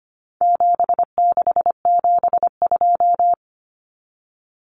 Morse code, heard as a keyed tone, about 25 wpm, 700 Hz. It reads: 7672
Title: Common Cut Numbers - 4-Digit 25wpm